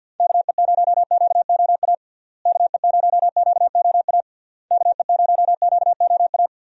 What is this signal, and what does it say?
Morse code, heard as a keyed tone, about 51 wpm, 700 Hz. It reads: KE0QOA KE0QOA KE0QOA
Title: USA Call Signs - 6 Characters - Repeated 3x - 50wpm